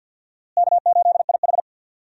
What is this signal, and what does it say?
Morse code, heard as a keyed tone, about 52 wpm, 700 Hz. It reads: K8IH
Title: USA Call Signs - 4 Characters - Repeated 2x - 50wpm